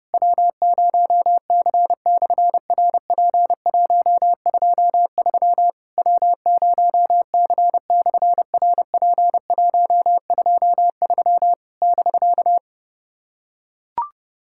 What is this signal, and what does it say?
Morse code, heard as a keyed tone, about 31 wpm, 700 Hz. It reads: W0C/RP123 W0C/RP123 <BK>
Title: SOTA QSO Practice 30wpm